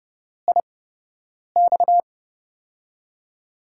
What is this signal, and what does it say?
Morse code, heard as a keyed tone, about 30 wpm, spaced as for 8 wpm, 700 Hz. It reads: IX
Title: USA Call Signs - 8x Character Spacing - 30wpm